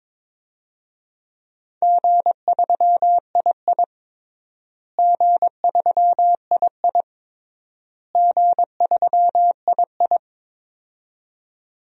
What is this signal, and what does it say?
Morse code, heard as a keyed tone, about 22 wpm, 700 Hz. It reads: G3II G3II G3II
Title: DXCC-100 Call Signs - 4 Characters - Repeated 3x - 22wpm